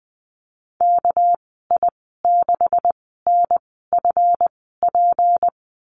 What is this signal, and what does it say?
Morse code, heard as a keyed tone, about 20 wpm, spaced as for 14 wpm, 700 Hz. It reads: KI6NFP
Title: 200 Lesson 32g - Letter k - Call Signs 20wpm